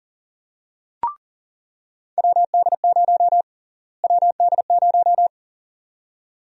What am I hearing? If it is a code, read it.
Morse code, heard as a keyed tone, about 40 wpm, 700 Hz. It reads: WD0 WD0
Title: USA Call Sign Prefixes - Repeated 2x - 40wpm